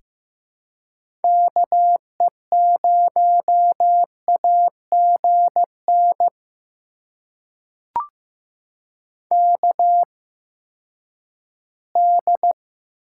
Morse code, heard as a keyed tone, about 15 wpm, spaced as for 10 wpm, 700 Hz. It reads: KE0AGN KD
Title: USA Call Signs - 6 Characters - 8x Character Spacing - 15wpm